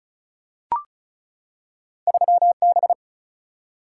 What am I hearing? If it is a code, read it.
Morse code, heard as a keyed tone, about 36 wpm, 700 Hz. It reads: 3 B
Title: Single-Letter Single-Number 35wpm